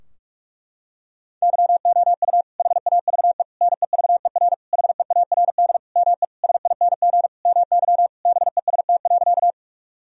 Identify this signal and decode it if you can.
Morse code, heard as a keyed tone, about 46 wpm, 700 Hz. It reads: YOU HAVE NEVER HEARD ME SING MY BEST.
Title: Sentences from Top 1000 Words - Review Part 2 - 45wpm